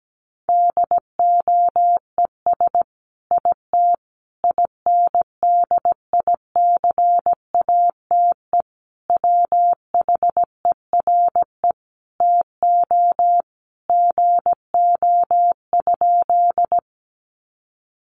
Morse code, heard as a keyed tone, about 17 wpm, 700 Hz. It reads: DOES IT INDICATE WHERE TO GO?
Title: Sentences from Top 1300 Words 17wpm